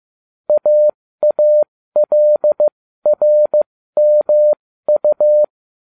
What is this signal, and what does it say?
Morse code, heard as a keyed tone, about 15 wpm, spaced as for 13 wpm, 600 Hz. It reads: AALRMU